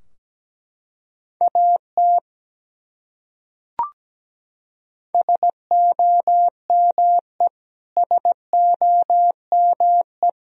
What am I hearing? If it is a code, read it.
Morse code, heard as a keyed tone, about 17 wpm, 700 Hz. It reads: AT SOME SOME
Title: Top 100 Words Repeated 17wpm